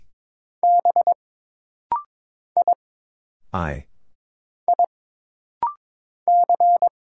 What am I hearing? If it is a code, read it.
Morse code, heard as a keyed tone, about 22 wpm, 700 Hz. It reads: B I I C
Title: Single Letters - Rapid-Fire 22wpm